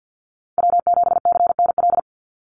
Morse code, heard as a keyed tone, about 48 wpm, 700 Hz. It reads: W7CNL
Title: NA SOTA Chaser Callsigns 50wpm